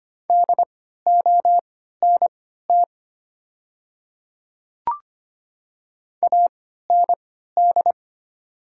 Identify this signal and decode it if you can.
Morse code, heard as a keyed tone, about 25 wpm, spaced as for 14 wpm, 700 Hz. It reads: DONT AND
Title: 3x Character Spacing - Top 300 Words - 25wpm